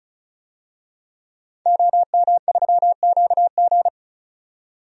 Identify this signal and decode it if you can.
Morse code, heard as a keyed tone, about 36 wpm, 700 Hz. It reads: OM3QG
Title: DXCC-100 Call Signs - 5 Characters - 35wpm